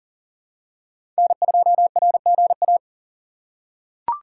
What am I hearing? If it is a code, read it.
Morse code, heard as a keyed tone, about 37 wpm, 700 Hz. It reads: N2RGA
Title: USA Call Signs - 5 Characters - 4x Character Spacing - 40wpm